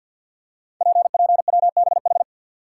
Morse code, heard as a keyed tone, about 48 wpm, 690 Hz. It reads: PPWBH
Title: Code Groups - 5 Letters - 50wpm